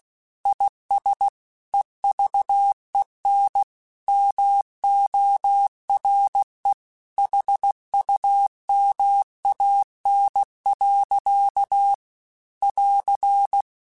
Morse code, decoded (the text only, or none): IS EVEN MORE HUMAN. <AR>